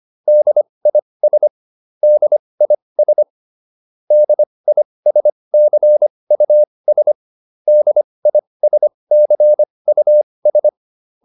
Morse code, heard as a keyed tone, about 25 wpm, spaced as for 20 wpm, 600 Hz. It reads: DIS DIS DISCUS DISCUS